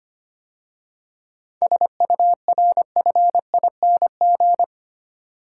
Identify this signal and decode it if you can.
Morse code, heard as a keyed tone, about 29 wpm, 700 Hz. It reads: SURFING
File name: Easy Words - Speed-Racing with Hints - 25wpm